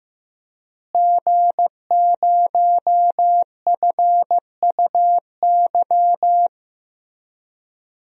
Morse code, heard as a keyed tone, about 15 wpm, 700 Hz. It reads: G0FUY
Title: DXCC-100 Call Signs - 5 Characters - Repeated 2x - 15wpm